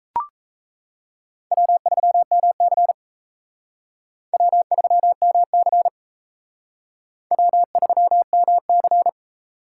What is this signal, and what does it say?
Morse code, heard as a keyed tone, about 38 wpm, 700 Hz. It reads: W3MC W3MC W3MC